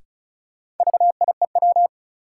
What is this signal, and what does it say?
Morse code, heard as a keyed tone, about 36 wpm, 700 Hz. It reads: VIEW
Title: Sets of 5 Words - New Word Drill - 35wpm